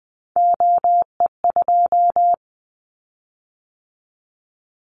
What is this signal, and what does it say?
Morse code, heard as a keyed tone, about 20 wpm, 700 Hz. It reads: OE2
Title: Callsign Prefixes for Top 50 DXCC - 20wpm